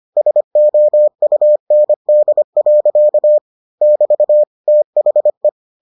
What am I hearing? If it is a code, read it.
Morse code, heard as a keyed tone, about 25 wpm, 600 Hz. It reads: SOUND. <BT> THE